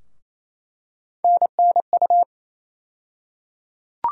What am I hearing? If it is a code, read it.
Morse code, heard as a keyed tone, about 28 wpm, 700 Hz. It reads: NNU
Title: USA Call Sign Postfixes - 3 Letters 28wpm